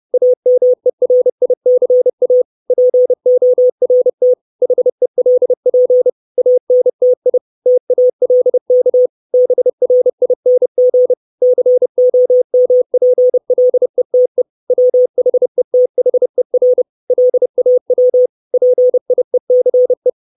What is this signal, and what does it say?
Morse code, heard as a keyed tone, about 30 wpm, 500 Hz. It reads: AMERICA PORT HELP ANTI TALK BRING COMPLETE WHETHER LAW PIECE
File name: 632 Words 30WPM